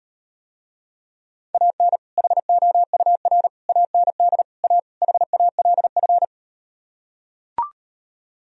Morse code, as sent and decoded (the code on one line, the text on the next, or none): .- -. / .... --- ..- .-. / .- -. -.. / .- / .... .- .-.. ..-.
AN HOUR AND A HALF